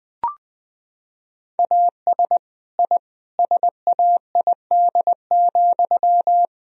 Morse code, heard as a keyed tone, about 20 wpm, 700 Hz. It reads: AS I SAID,
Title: Sentences from Top 1000 Words - Review Part 3 - 20wpm